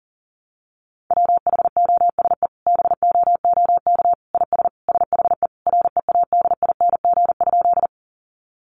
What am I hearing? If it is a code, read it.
Morse code, heard as a keyed tone, about 40 wpm, 700 Hz. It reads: WHOSE BOOK IS SHE READING?
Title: Sentences from Top 1200 Words 40wpm